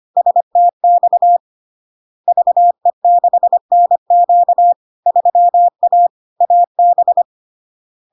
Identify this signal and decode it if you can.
Morse code, heard as a keyed tone, about 25 wpm, 700 Hz. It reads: STX VE6NQ 3A AB
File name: LARK-FD-2022-Log-in-CW-25wpm-700-Hz